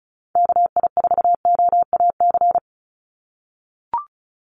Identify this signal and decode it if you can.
Morse code, heard as a keyed tone, about 34 wpm, 700 Hz. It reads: KI4OAC
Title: USA Call Signs 35wpm